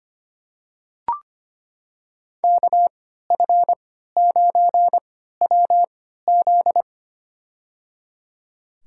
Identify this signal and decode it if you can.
Morse code, heard as a keyed tone, about 25 wpm, spaced as for 14 wpm, 700 Hz. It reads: KF9WZ
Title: USA Call Signs - 5 Characters - 3x Character Spacing - 25wpm